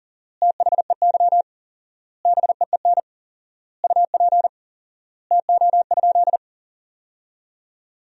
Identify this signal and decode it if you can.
Morse code, heard as a keyed tone, about 37 wpm, 700 Hz. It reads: THEY BEEN UP TO?
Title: 4x Word-Spacing - Sentences from Top 100 Words - 40wpm